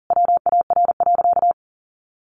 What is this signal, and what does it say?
Morse code, heard as a keyed tone, about 40 wpm, 700 Hz. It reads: WAR.
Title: Sentences from Top 1400 Words 40wpm